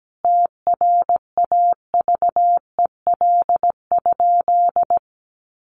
Morse code, heard as a keyed tone, about 17 wpm, 700 Hz. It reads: TRAVEL?
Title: Sentences from Top 1400 Words 17wpm